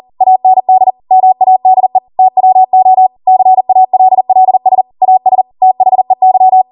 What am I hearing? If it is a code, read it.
Morse code, heard as a keyed tone, about 40 wpm, 750 Hz. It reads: AND MADE TWO CALLS AS THEY